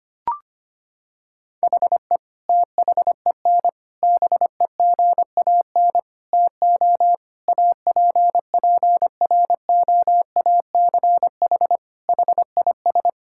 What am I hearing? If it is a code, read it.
Morse code, heard as a keyed tone, about 25 wpm, 700 Hz. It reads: HE THEN BEGAN TO APPROACH HIS
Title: Sentences from Top 1000 Words - Review Part 6 - 25wpm